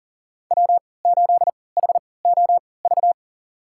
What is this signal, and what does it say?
Morse code, heard as a keyed tone, about 40 wpm, spaced as for 23 wpm, 700 Hz. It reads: W8HOV
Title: USA Call Signs - 3x Character Spacing - 40wpm